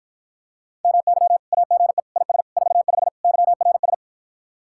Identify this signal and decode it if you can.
Morse code, heard as a keyed tone, about 52 wpm, 700 Hz. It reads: MY AGE IS 45 YRS